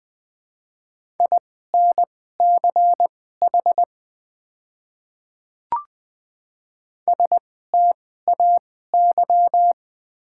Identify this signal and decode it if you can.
Morse code, heard as a keyed tone, about 20 wpm, spaced as for 14 wpm, 700 Hz. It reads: INCH STAY
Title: 2x Character Spacing - Top 500 Words - 20wpm